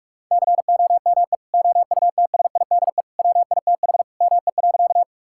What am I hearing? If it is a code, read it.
Morse code, heard as a keyed tone, about 41 wpm, 700 Hz. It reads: COME OUTSIDE WITH ME.
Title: Sentences from Top 1000 Words - Review Part 6 - 45wpm